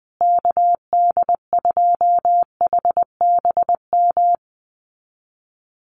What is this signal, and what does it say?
Morse code, heard as a keyed tone, about 20 wpm, 700 Hz. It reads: KD2HBM
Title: USA Call Signs Repeated 3x 20wpm